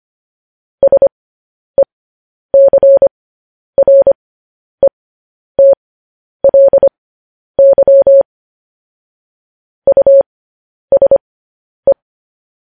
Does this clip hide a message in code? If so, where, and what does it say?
Yes: Morse code from 0.8 to 11.9 s, heard as a keyed tone, about 25 wpm, spaced as for 10 wpm, 550 Hz. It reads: SECRETLY USE